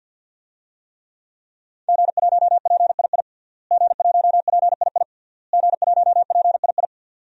Morse code, heard as a keyed tone, about 48 wpm, 700 Hz. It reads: G1PII G1PII G1PII
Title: DXCC-100 Call Signs - 5 Characters - Repeated 3x - 50wpm